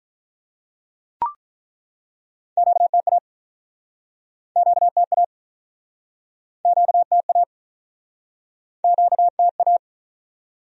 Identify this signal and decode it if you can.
Morse code, heard as a keyed tone, about 49 wpm, 700 Hz. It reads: QTA QTA QTA QTA